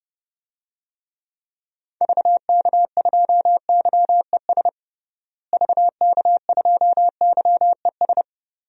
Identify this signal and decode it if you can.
Morse code, heard as a keyed tone, about 30 wpm, 700 Hz. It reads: VK2YES VK2YES
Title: DXCC-100 Call Signs - 6 Characters - Repeated 2x - 30wpm